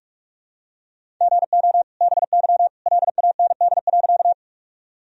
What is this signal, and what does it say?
Morse code, heard as a keyed tone, about 46 wpm, 700 Hz. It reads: GO BY LAND.
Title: Sentences from Top 1000 Words - Review Part 1 - 45wpm